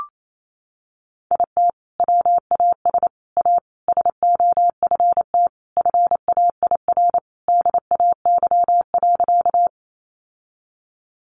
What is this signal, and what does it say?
Morse code, heard as a keyed tone, about 28 wpm, 700 Hz. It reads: IT WAS A SOFT FAIR DAY.
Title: Sentences from Top 1500 Words 28wpm